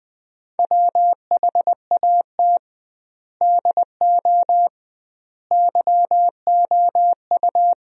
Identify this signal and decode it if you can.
Morse code, heard as a keyed tone, about 20 wpm, 700 Hz. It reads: WHAT DO YOU
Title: 2x Word-Spacing - Sentences from Top 200 Words - 20wpm